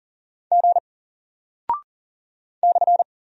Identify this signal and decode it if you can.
Morse code, heard as a keyed tone, about 38 wpm, 700 Hz. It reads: G /
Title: Single Character (Letter:Number:Punctuation:Prosign) 40wpm